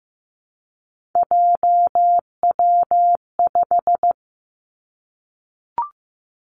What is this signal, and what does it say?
Morse code, heard as a keyed tone, about 15 wpm, 700 Hz. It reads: JW5
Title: Three Challenging Letter-Number Combinations 15wpm